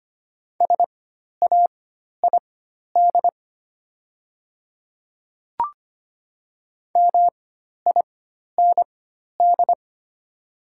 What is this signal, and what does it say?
Morse code, heard as a keyed tone, about 25 wpm, spaced as for 12 wpm, 690 Hz. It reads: SAID MIND
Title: 4x Character Spacing - Top 500 Words - 25wpm